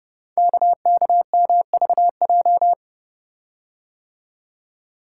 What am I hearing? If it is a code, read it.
Morse code, heard as a keyed tone, about 30 wpm, 700 Hz. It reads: KKMVJ
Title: Code Groups - 5 Letters - 30wpm